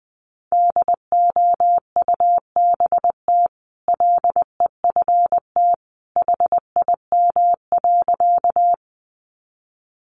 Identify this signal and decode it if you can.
Morse code, heard as a keyed tone, about 20 wpm, 700 Hz. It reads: DOUBT LEFT HIM.
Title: Sentences from Top 1500 Words 20wpm